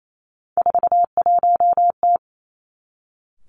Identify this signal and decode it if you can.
Morse code, heard as a keyed tone, about 28 wpm, 700 Hz. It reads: 41T
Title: Three Challenging Letter-Number Combinations 28wpm